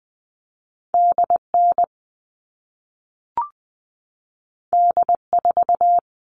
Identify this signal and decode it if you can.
Morse code, heard as a keyed tone, about 20 wpm, 700 Hz. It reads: DN D4
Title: All Cut Numbers - 2-Digit 20wpm